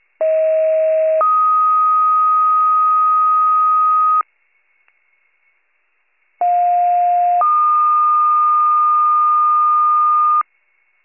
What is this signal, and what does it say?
Two-tone page, heard as a one-second tone, then a three-second tone, about 640 then 1250 Hz, about 710 then 1220 Hz. It is heard from 0.2 to 10.4 s.